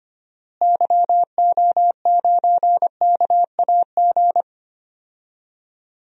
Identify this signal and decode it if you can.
Morse code, heard as a keyed tone, about 25 wpm, 700 Hz. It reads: YO9KAG
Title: DXCC-100 Call Signs - 6 Characters - Repeated 3x - 25wpm